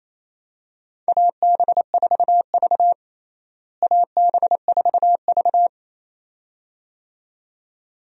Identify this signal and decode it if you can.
Morse code, heard as a keyed tone, about 28 wpm, 700 Hz. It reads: AB4V AB4V
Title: USA Call Signs - 4 Characters - Repeated 2x - 28wpm